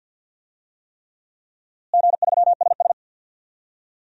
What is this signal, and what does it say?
Morse code, heard as a keyed tone, about 52 wpm, 700 Hz. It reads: G3SS
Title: DXCC-100 Call Signs - 4 Characters - Repeated 3x - 50wpm